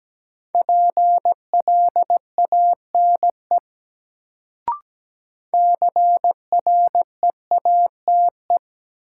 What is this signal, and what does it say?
Morse code, heard as a keyed tone, about 17 wpm, 700 Hz. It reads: PLANE CREATE
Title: Sentences from Top 1000 Words - Review Part 3 - New Word Drill - 17wpm